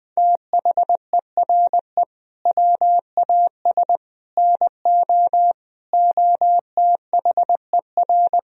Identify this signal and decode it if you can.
Morse code, heard as a keyed tone, about 20 wpm, 700 Hz. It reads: THERE WAS NO OTHER
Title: Sentences from Top 1000 Words - Review Part 3 - 20wpm